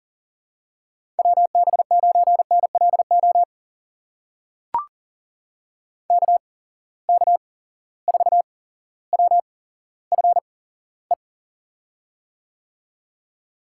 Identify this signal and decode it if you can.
Morse code, heard as a keyed tone, about 37 wpm, spaced as for 17 wpm, 700 Hz. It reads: WB9NLO KK4WFE
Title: USA Call Signs - 6 Characters - 8x Character Spacing - 40wpm